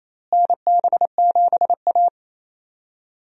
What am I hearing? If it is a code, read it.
Morse code, heard as a keyed tone, about 28 wpm, 700 Hz. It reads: NB7A
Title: USA Call Signs - 4 Characters - Repeated 2x - 28wpm